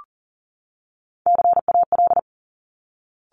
Morse code, heard as a keyed tone, about 38 wpm, 700 Hz. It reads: CAL
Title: Top Three Letter Combinations in Words 40wpm